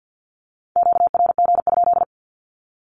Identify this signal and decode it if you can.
Morse code, heard as a keyed tone, about 48 wpm, 700 Hz. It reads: QRG?
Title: Q-Codes 50wpm